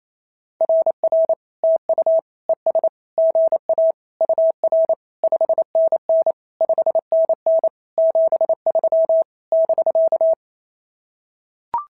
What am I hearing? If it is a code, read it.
Morse code, heard as a keyed tone, about 28 wpm, 690 Hz. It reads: RR TU ES GA UR 5NN 5NN 73 <BK>
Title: SOTA QSO Practice 28wpm